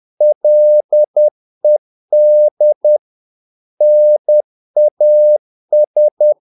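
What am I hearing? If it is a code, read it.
Morse code, heard as a keyed tone, about 10 wpm, 600 Hz. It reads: LED NAS
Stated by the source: Morse practice of short Danish words at 10 wpm, from CW med Gnister.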